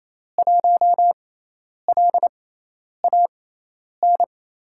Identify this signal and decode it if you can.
Morse code, heard as a keyed tone, about 27 wpm, spaced as for 10 wpm, 700 Hz. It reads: 1LAN
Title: USA Call Signs - 5 Characters - 6x Character Spacing - 28wpm